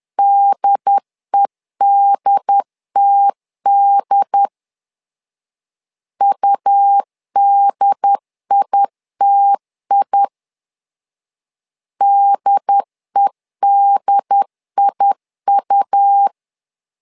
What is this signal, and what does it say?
Morse code, heard as a keyed tone, about 10 wpm, 800 Hz. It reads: DEDTD UDITI DEDIU
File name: LEZIONI / 2 UHDN / 2 UHD 10wpm 1826